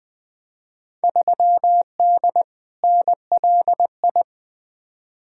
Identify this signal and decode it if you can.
Morse code, heard as a keyed tone, about 21 wpm, 700 Hz. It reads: 3D NLI
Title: Field Day Exchanges Encore - 20wpm